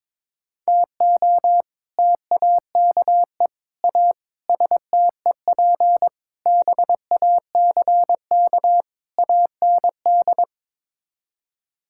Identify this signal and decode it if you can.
Morse code, heard as a keyed tone, about 22 wpm, 700 Hz. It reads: TO TAKE A STEP BACK AND
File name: practice / 22wpm / Sets of 6 Words - 22wpm